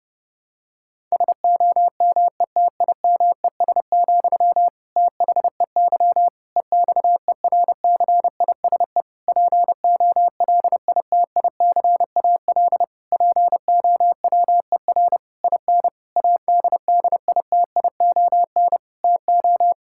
Morse code, heard as a keyed tone, about 30 wpm, 700 Hz. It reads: SOMETIMES, THEY EXERCISE POLITICAL POWER IN ADDITION TO
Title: Sentences from Top 1000 Words - Review Part 6 - 30wpm